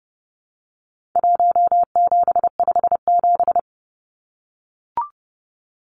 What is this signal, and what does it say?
Morse code, heard as a keyed tone, about 30 wpm, 700 Hz. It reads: 1757
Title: Numbers - Contest Style - 4-Digit 30wpm